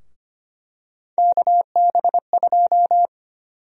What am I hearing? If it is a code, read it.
Morse code, heard as a keyed tone, about 25 wpm, 700 Hz. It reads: KB2
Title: USA Call Sign Prefixes - 25wpm